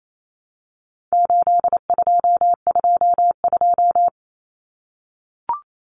Morse code, heard as a keyed tone, about 28 wpm, 700 Hz. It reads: 8222
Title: Common Cut Numbers - 4-Digit 28wpm